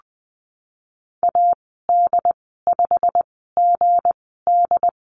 Morse code, heard as a keyed tone, about 20 wpm, spaced as for 15 wpm, 700 Hz. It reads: AD5GD
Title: 179 Lesson 29g - Slash - Call Signs 20wpm